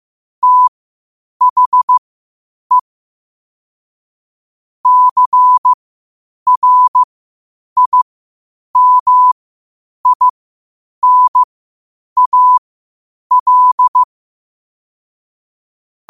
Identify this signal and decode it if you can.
Morse code, heard as a keyed tone, about 15 wpm, spaced as for 8 wpm, 1000 Hz. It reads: THE CRIMINAL
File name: 8wpm